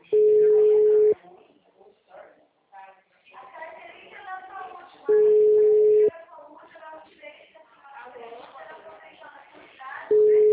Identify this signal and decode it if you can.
Ringback tone (Europe, 425 Hz), with speech.
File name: out- -2009- - - .81722